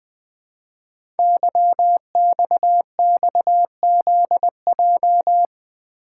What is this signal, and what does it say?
Morse code, heard as a keyed tone, about 20 wpm, 700 Hz. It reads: YXXZJ
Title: Code Groups - 5 Letters - 20wpm